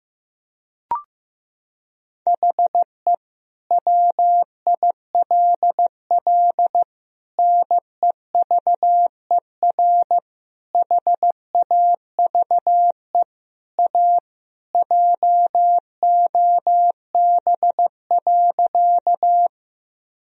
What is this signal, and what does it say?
Morse code, heard as a keyed tone, about 15 wpm, 700 Hz. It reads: HE WILL NEVER HAVE A JOB.
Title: Sentences from Top 1000 Words - Review Part 3 - 15wpm